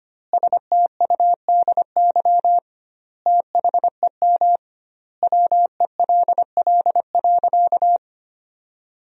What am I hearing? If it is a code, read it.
Morse code, heard as a keyed tone, about 25 wpm, 700 Hz. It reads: STUDY THEM WELL.
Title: 2x Word-Spacing - Sentences from Top 300 Words - 25wpm